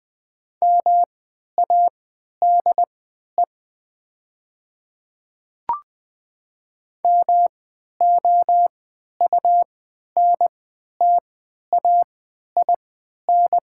Morse code, heard as a keyed tone, about 20 wpm, spaced as for 11 wpm, 690 Hz. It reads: MADE MOUNTAIN
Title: 3x Character Spacing - Top 300 Words - 20wpm